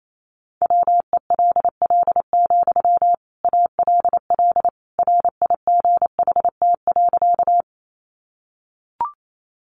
Morse code, heard as a keyed tone, about 28 wpm, 700 Hz. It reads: WELL, ALL RIGHT.
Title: Sentences from Top 200 Words 28wpm